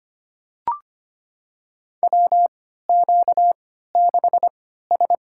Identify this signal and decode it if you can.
Morse code, heard as a keyed tone, about 25 wpm, spaced as for 14 wpm, 700 Hz. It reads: WQ6S
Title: USA Call Signs - 3x Character Spacing - 25wpm